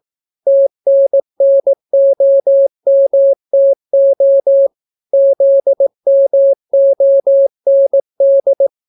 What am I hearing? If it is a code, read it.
Morse code, heard as a keyed tone, about 18 wpm, 550 Hz. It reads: TNNOMTO ZMOND